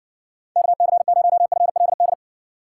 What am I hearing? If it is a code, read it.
Morse code, heard as a keyed tone, about 60 wpm, 700 Hz. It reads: KC0FLD